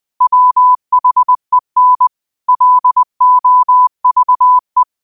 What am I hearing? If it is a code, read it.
Morse code, heard as a keyed tone, about 20 wpm, 1000 Hz. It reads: WHEN LOVE
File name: qotdfiles / 20wpm